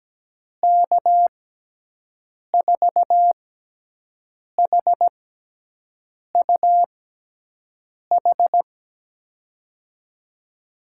Morse code, heard as a keyed tone, about 17 wpm, spaced as for 6 wpm, 700 Hz. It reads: K4HUH
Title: USA Call Signs - 6 Characters - 6x Character Spacing - 17wpm